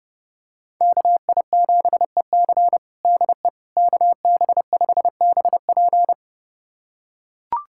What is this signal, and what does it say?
Morse code, heard as a keyed tone, about 31 wpm, 700 Hz. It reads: KI7EC DE KB5BP
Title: USA Callsign DE Callsign - 5 Characters - 30wpm